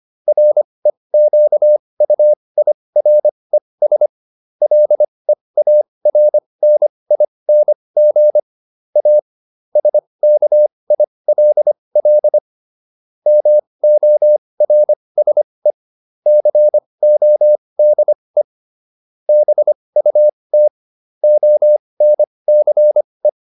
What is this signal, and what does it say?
Morse code, heard as a keyed tone, about 25 wpm, spaced as for 20 wpm, 600 Hz. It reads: REQUIRES LEARNING A SKILL MORSE CODE BUT ONCE